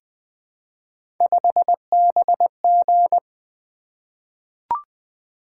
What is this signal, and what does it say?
Morse code, heard as a keyed tone, about 20 wpm, 700 Hz. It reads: 5BG
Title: Three Letter-Number Combinations - 20wpm